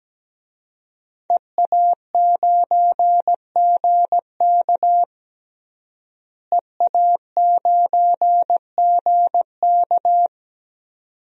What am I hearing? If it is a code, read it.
Morse code, heard as a keyed tone, about 17 wpm, 700 Hz. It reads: EA9GK EA9GK
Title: DXCC-100 Call Signs - 5 Characters - Repeated 2x - 17wpm